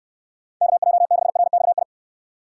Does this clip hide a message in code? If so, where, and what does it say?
Yes: Morse code from 0.6 to 1.8 s, heard as a keyed tone, about 69 wpm, 700 Hz. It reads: KJ6RXI